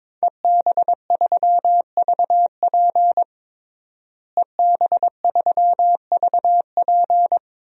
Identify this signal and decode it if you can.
Morse code, heard as a keyed tone, about 22 wpm, 700 Hz. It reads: EB3VP EB3VP
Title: DXCC-100 Call Signs - 5 Characters - Repeated 3x - 22wpm